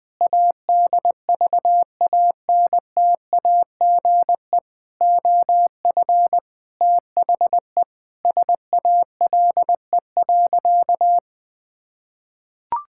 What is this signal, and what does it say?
Morse code, heard as a keyed tone, about 20 wpm, 700 Hz. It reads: ADVANTAGE OF THE SALE.
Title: Sentences with 5-Word Phrases - Spoken Beginning - 20wpm